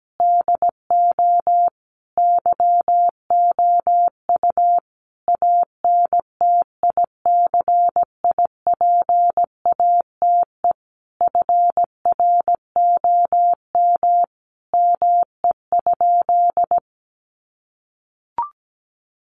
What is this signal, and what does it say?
Morse code, heard as a keyed tone, about 17 wpm, 700 Hz. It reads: DO YOU ANTICIPATE FROM ME?
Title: Sentences from Top 1500 Words 17wpm